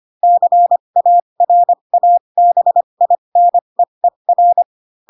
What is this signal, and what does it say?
Morse code, heard as a keyed tone, about 25 wpm, spaced as for 22 wpm, 700 Hz. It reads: CARABINEER
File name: Pinocchio / Pinocchio ch3 18 25 ik1hge